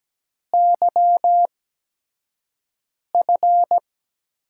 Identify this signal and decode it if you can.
Morse code, heard as a keyed tone, about 17 wpm, spaced as for 5 wpm, 700 Hz. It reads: YF
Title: USA Call Signs - 6 Characters - 8x Character Spacing - 17wpm